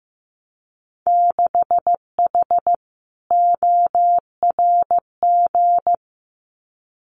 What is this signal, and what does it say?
Morse code, heard as a keyed tone, about 15 wpm, 700 Hz. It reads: 6H ORG
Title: Winter Field Day Exchanges Encore 15wpm